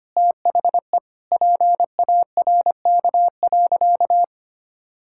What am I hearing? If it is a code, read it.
Morse code, heard as a keyed tone, about 25 wpm, 700 Hz. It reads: THE PARK.
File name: practice / 25wpm / Sentences with 2-Word Phrases - 25wpm